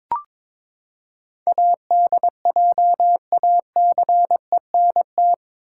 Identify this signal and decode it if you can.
Morse code, heard as a keyed tone, about 21 wpm, 700 Hz. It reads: ADJACENT
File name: practice / 22wpm / Eight Letter Words - 22wpm